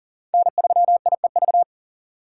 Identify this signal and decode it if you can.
Morse code, heard as a keyed tone, about 40 wpm, 700 Hz. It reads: N3IEV
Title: USA Call Signs - 2x Character Spacing - 40wpm